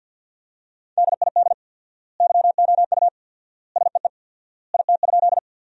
Morse code, heard as a keyed tone, about 50 wpm, 700 Hz. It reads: DID YOU SEE IT?
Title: 4x Word-Spacing - Sentences from Top 100 Words - 50wpm